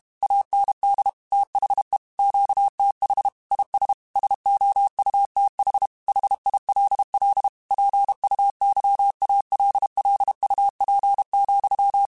Morse code, heard as a keyed tone, about 32 wpm, 800 Hz. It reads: AND THE QTH IS SOUTH HILL PUYALLUP,